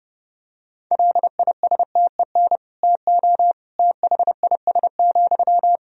Morse code, heard as a keyed tone, about 32 wpm, 700 Hz. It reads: LISTEN TO THIS,
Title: Sentences from Top 400 Words and Abbreviations 30wpm